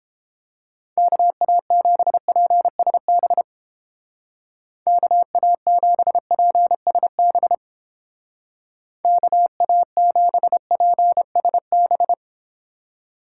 Morse code, heard as a keyed tone, about 29 wpm, 700 Hz. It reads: KA7PSB KA7PSB KA7PSB